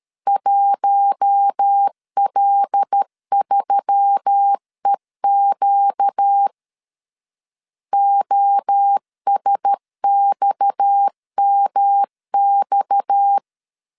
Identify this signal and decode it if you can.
Morse code, heard as a keyed tone, about 13 wpm, 800 Hz. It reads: 1L3EQ OSXMX
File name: LEZIONI / 6 PKX0 / 6 PKX 12wpm 2024